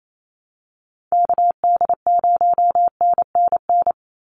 Morse code, heard as a keyed tone, about 28 wpm, 700 Hz. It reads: KD0NNN
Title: USA Call Signs 28wpm